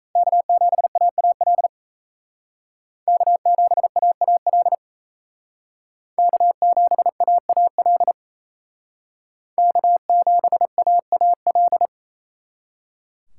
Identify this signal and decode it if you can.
Morse code, heard as a keyed tone, about 36 wpm, 700 Hz. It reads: K7AAL K7AAL K7AAL K7AAL